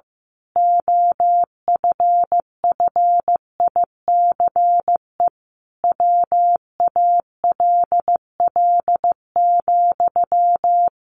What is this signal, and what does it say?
Morse code, heard as a keyed tone, about 15 wpm, 700 Hz. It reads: OFFICE WALL,
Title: A Sound Of Thunder 15wpm